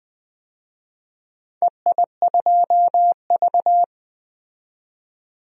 Morse code, heard as a keyed tone, about 20 wpm, 700 Hz. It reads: EI2V
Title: DXCC-100 Call Signs - 4 Characters - 20wpm